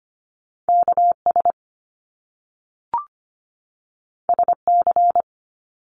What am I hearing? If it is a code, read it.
Morse code, heard as a keyed tone, about 25 wpm, 700 Hz. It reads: KS SC
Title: US States 25wpm